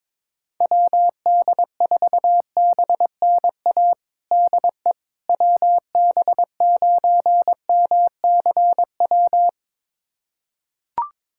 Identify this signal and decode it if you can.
Morse code, heard as a keyed tone, about 22 wpm, 700 Hz. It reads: WD4BNA DE WB9MCW
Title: USA Callsign DE Callsign - 6 Characters - 22wpm